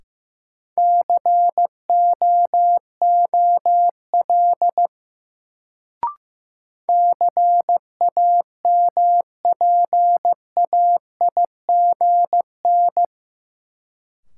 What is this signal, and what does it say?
Morse code, heard as a keyed tone, about 15 wpm, 700 Hz. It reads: COOL CAMPAIGN
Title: Sentences from Top 1200 Words - New Word Drill - 15wpm